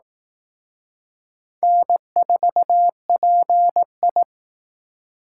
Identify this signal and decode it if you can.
Morse code, heard as a keyed tone, about 19 wpm, 700 Hz. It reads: N4PI
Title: Speed Racing - USA 1x2 Call Signs 15wpm